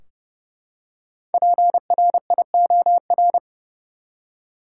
Morse code, heard as a keyed tone, about 30 wpm, 700 Hz. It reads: PRIOR
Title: Five Letter Words - 30wpm